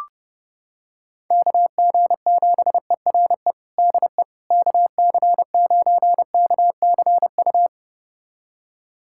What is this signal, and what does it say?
Morse code, heard as a keyed tone, about 32 wpm, 700 Hz. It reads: KG7ERE DE KC9KCU
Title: USA Callsign DE Callsign 30wpm